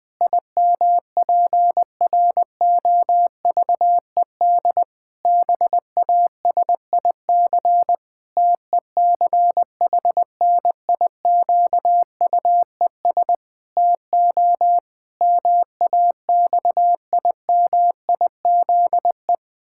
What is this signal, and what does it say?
Morse code, heard as a keyed tone, about 20 wpm, 700 Hz. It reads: IMPROVED BASIC TECHNIQUES TO MAXIMIZE
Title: Pangram Challenge v2 - 20wpm